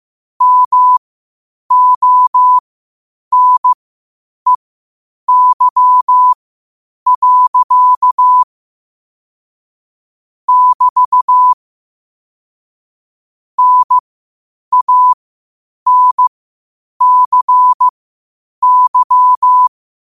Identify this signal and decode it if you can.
Morse code, heard as a keyed tone, about 15 wpm, spaced as for 8 wpm, 1000 Hz. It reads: MONEY. <BT> NANCY
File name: qotdfiles / 8wpm